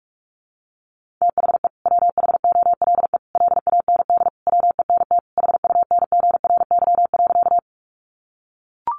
Morse code, heard as a keyed tone, about 44 wpm, 700 Hz. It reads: THE WHOLE LAND WENT HUNGRY.
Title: Sentences from Top 1400 Words 45wpm